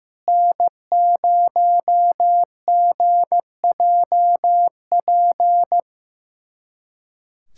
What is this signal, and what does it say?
Morse code, heard as a keyed tone, about 15 wpm, 690 Hz. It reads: N0GJP
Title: SOTA QSO Practice 15wpm